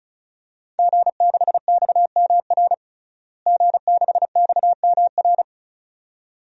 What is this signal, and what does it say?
Morse code, heard as a keyed tone, about 34 wpm, 700 Hz. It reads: G6XMR G6XMR
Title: DXCC-100 Call Signs - 5 Characters - Repeated 2x - 35wpm